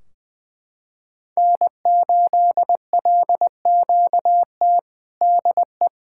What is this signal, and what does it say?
Morse code, heard as a keyed tone, about 20 wpm, 700 Hz. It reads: N8LQT DE
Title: Practice QSO - Exchange 4 - Ending - 20wpm